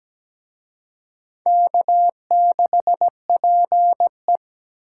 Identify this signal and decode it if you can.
Morse code, heard as a keyed tone, about 20 wpm, 700 Hz. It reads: K6PE
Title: Speed Racing - USA 1x2 Call Signs 17wpm